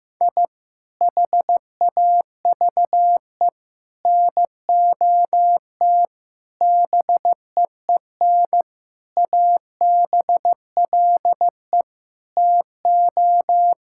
Morse code, heard as a keyed tone, about 15 wpm, 700 Hz. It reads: I HAVE NOT BEEN ABLE TO
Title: Sentences from Top 1000 Words - Review Part 6 - 15wpm